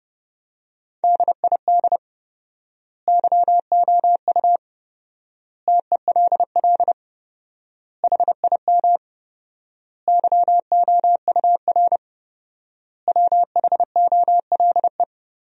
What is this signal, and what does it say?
Morse code, heard as a keyed tone, about 30 wpm, 700 Hz. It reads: DID YOU TELL HIM YOUR WHOLE
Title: 4x Word-Spacing - Sentences from Top 400 Words - 30wpm